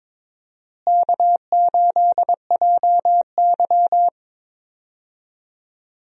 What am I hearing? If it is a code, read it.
Morse code, heard as a keyed tone, about 22 wpm, 700 Hz. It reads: K8JY
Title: USA Call Signs - 4 Characters - Repeated 2x - 22wpm